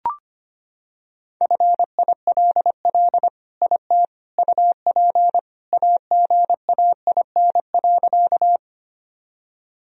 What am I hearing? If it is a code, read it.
Morse code, heard as a keyed tone, about 25 wpm, 700 Hz. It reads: FILL IT UP AGAIN.
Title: Sentences from Top 1000 Words - Review Part 3 - 25wpm